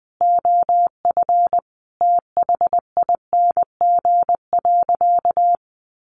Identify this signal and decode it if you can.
Morse code, heard as a keyed tone, about 20 wpm, 700 Hz. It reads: OF THING.
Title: Sentences from Top 1300 Words 20wpm